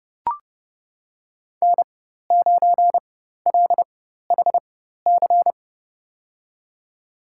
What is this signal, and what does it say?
Morse code, heard as a keyed tone, about 30 wpm, spaced as for 14 wpm, 700 Hz. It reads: N9LHC
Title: USA Call Signs - 5 Characters - 4x Character Spacing - 30wpm